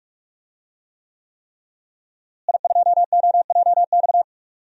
Morse code, heard as a keyed tone, about 44 wpm, 700 Hz. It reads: I2OJX
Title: DXCC-100 Call Signs - 5 Characters - Repeated 3x - 45wpm